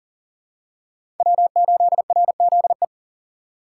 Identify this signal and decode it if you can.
Morse code, heard as a keyed tone, about 40 wpm, 700 Hz. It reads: W8RZE
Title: USA Call Signs - 5 Characters - 6x Character Spacing - 40wpm